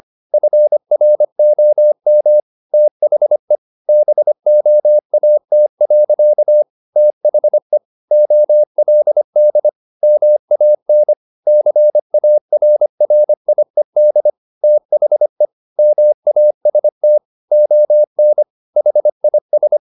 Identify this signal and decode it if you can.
Morse code, heard as a keyed tone, about 25 wpm, 600 Hz. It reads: FROM THE BOAT. THE OLD MAN CARRIED THE MAST ON HIS
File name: the old man and the sea part 1 25wpm